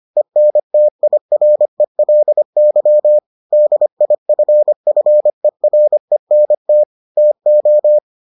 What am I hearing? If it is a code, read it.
Morse code, heard as a keyed tone, about 25 wpm, 600 Hz. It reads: ENTIRELY DIFFERENT TO